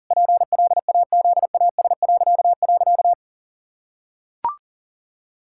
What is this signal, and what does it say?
Morse code, heard as a keyed tone, about 40 wpm, 700 Hz. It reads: PLAZAS..
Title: Pangram Challenge v2 - 40wpm